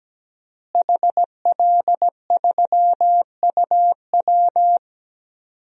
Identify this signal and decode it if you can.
Morse code, heard as a keyed tone, about 17 wpm, 700 Hz. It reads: HL3UW
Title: DXCC-100 Call Signs - 5 Characters - Repeated 2x - 17wpm